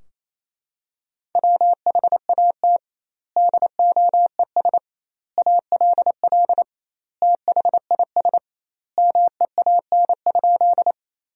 Morse code, heard as a keyed tone, about 28 wpm, 700 Hz. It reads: WHAT DOES ALL THIS MEAN?
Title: 2x Word-Spacing - Sentences from Top 200 Words - 28wpm